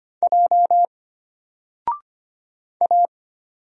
Morse code, heard as a keyed tone, about 25 wpm, 700 Hz. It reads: J A
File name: Single-Letter-Number-25wpm